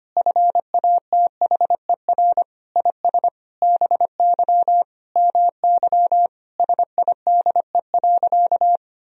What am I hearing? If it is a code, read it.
Morse code, heard as a keyed tone, about 25 wpm, 700 Hz. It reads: FATHER IS BY MY SIDE.
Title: Sentences from Top 1000 Words - Review Part 1 - 25wpm